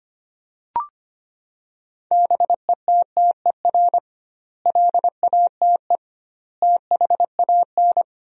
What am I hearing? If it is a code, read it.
Morse code, heard as a keyed tone, about 25 wpm, 700 Hz. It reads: BETTER LATE THAN
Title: 2x Word-Spacing - Sentences from Top 400 Words - 25wpm